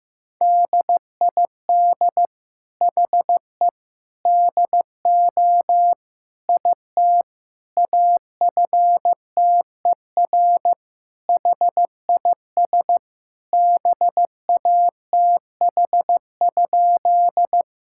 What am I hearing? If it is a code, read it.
Morse code, heard as a keyed tone, about 15 wpm, 700 Hz. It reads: DID HE DO IT AFTER HIS BATH?
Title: Sentences from Top 1600 Words 15wpm